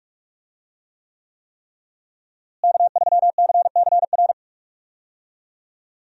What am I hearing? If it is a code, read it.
Morse code, heard as a keyed tone, about 41 wpm, 700 Hz. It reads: K3CCR
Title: USA Call Signs - 5 Characters - Repeated 3x - 45wpm